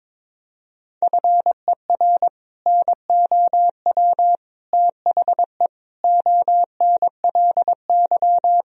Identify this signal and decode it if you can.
Morse code, heard as a keyed tone, about 22 wpm, 700 Hz. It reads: FER NOW THE ONLY
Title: Sentences from Top 200 Words and Abbreviations 22wpm